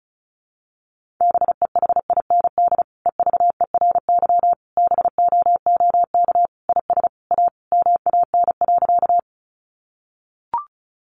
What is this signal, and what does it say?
Morse code, heard as a keyed tone, about 34 wpm, 700 Hz. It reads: BEHIND EVERY BOOK IS A MAN.
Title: Sentences from Top 500 Words 35wpm